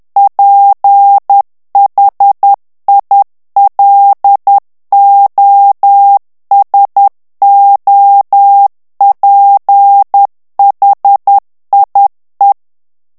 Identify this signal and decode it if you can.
Morse code, heard as a keyed tone, about 10 wpm, 800 Hz. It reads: PHILOSOPHIE
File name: FRANCAIS TEXTE CLAIR 2 VITESSE 10 MOTS MINUTES